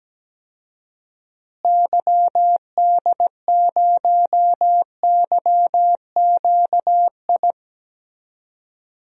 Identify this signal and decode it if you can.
Morse code, heard as a keyed tone, about 17 wpm, 700 Hz. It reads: YD0YQI
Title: DXCC-100 Call Signs - 6 Characters - Repeated 2x - 17wpm